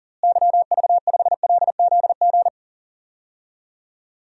Morse code, heard as a keyed tone, about 41 wpm, 700 Hz. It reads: YV5LZG
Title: DXCC-100 Call Signs - 6 Characters - Repeated 3x - 40wpm